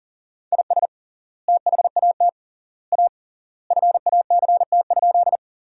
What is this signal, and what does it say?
Morse code, heard as a keyed tone, about 40 wpm, 700 Hz. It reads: IS THAT A FACT?
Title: 3x Word-Spacing - Sentences from Top 500 Words - 40wpm